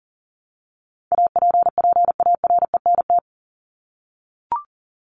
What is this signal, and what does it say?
Morse code, heard as a keyed tone, about 40 wpm, 700 Hz. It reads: APPARENT
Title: LY Suffix - Top 100 Words with Base Words 40wpm